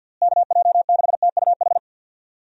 Morse code, heard as a keyed tone, about 49 wpm, 700 Hz. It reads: KJ6TUH
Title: USA Call Signs - 6 Characters - Repeated 3x - 50wpm